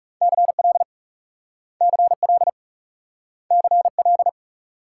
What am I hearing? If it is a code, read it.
Morse code, heard as a keyed tone, about 36 wpm, 700 Hz. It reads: CL CL CL
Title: Speed-Racing Multi-letter Phonemes - 30wpm